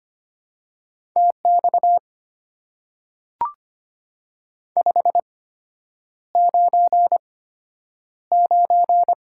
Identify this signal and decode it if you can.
Morse code, heard as a keyed tone, about 25 wpm, spaced as for 9 wpm, 700 Hz. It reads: TX 599
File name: Top 90 QSO Elements - 8x Character Spacing - 25wpm